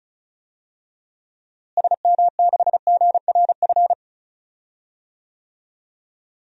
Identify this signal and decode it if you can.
Morse code, heard as a keyed tone, about 34 wpm, 700 Hz. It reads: SM6GRF
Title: DXCC-100 Call Signs - 6 Characters - Repeated 2x - 35wpm